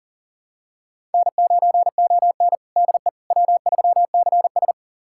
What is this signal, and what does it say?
Morse code, heard as a keyed tone, about 40 wpm, 700 Hz. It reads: N9ON DE W3CS
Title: USA Callsign DE Callsign - 4 Characters - 40wpm